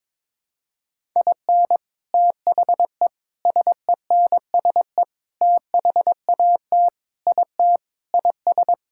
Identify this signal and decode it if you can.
Morse code, heard as a keyed tone, about 22 wpm, 700 Hz. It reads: IN THE SENSE THAT IT IS
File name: practice / 22wpm / Sentences with 4-Word Phrases - 22wpm